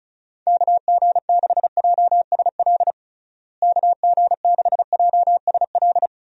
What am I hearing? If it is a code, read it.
Morse code, heard as a keyed tone, about 36 wpm, 700 Hz. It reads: KG6JSL KG6JSL
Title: USA Call Signs - 6 Characters - Repeated 3x - 35wpm